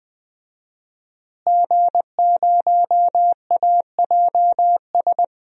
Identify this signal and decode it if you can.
Morse code, heard as a keyed tone, about 20 wpm, 700 Hz. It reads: G0AJS
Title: DXCC-100 Call Signs - 5 Characters - Repeated 3x - 20wpm